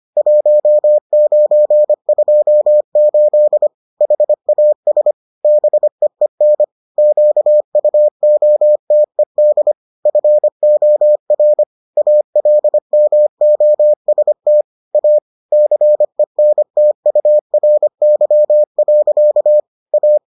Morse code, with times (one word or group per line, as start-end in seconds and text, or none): 0.2-3.7 1928
4.0-5.1 HAS
5.4-6.6 BEEN
7.0-9.7 QUOTED
10.1-11.6 FOR
12.0-14.6 ALMOST
14.9-15.2 A
15.5-19.6 CENTURY.
19.9-20.2 A